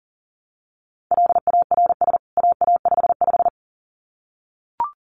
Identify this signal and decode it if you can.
Morse code, heard as a keyed tone, about 40 wpm, 700 Hz. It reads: LARS AA55
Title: CWT Exchange 40wpm